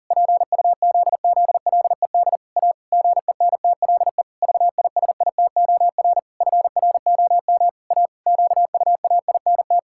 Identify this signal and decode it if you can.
Morse code, heard as a keyed tone, about 40 wpm, 700 Hz. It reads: PUZZLED A GENTLE VISITOR FROM A QUAINT
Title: Pangram Challenge v2 - 40wpm